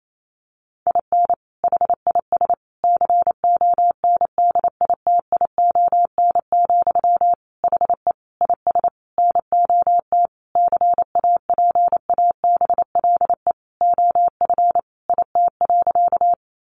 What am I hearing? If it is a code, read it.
Morse code, heard as a keyed tone, about 28 wpm, 700 Hz. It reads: IN HIS CONDITION, HE IS NOT CAPABLE OF IT.
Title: Sentences from Top 1200 Words 28wpm